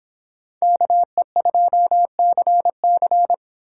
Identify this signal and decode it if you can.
Morse code, heard as a keyed tone, about 29 wpm, 700 Hz. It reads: KE2CC